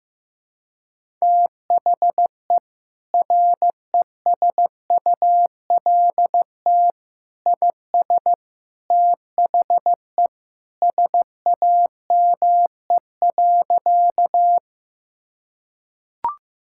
Morse code, heard as a keyed tone, about 14 wpm, 700 Hz. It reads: THE RESULT IS THE SAME.
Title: Sentences from Top 1000 Words - Review Part 5 - 15wpm